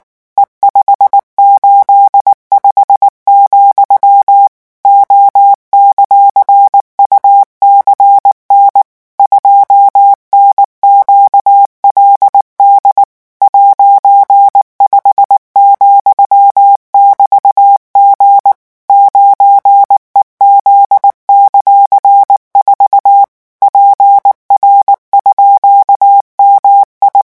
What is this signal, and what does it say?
Morse code, heard as a keyed tone, about 19 wpm, 800 Hz. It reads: E585, O;UCN 2NQLD '5,<BT>G 9EZ;4 PR_MI
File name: morse mp3 / revision14-vitesse 18 mots